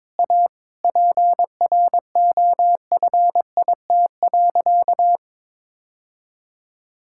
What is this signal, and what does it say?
Morse code, heard as a keyed tone, about 21 wpm, 700 Hz. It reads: A PROFIT.
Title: Sentences from Top 1000 Words - Review Part 5 - 22wpm